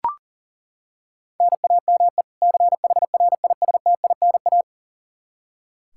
Morse code, heard as a keyed tone, about 40 wpm, 700 Hz. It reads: NAME CHRISTINA
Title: Practice QSO - Exchange 1 - Beginning - 40wpm